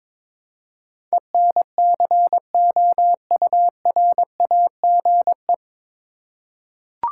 Morse code, heard as a keyed tone, about 22 wpm, 700 Hz. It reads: ENCOURAGE
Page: Morse Code Ninja practice 22wpm